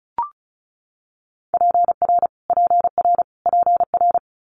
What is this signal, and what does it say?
Morse code, heard as a keyed tone, about 36 wpm, 700 Hz. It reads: PR PR PR
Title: Multi-letter Phonemes Repeated 3x 35wpm